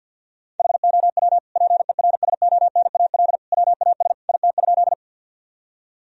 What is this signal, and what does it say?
Morse code, heard as a keyed tone, about 50 wpm, 700 Hz. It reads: HOW PERSONAL WAS IT?
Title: Sentences from Top 1000 Words - Review Part 4 - 50wpm